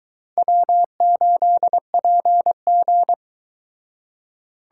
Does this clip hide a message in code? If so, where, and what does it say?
Yes: Morse code from 0.4 to 3.1 s, heard as a keyed tone, about 22 wpm, 700 Hz. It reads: W8PG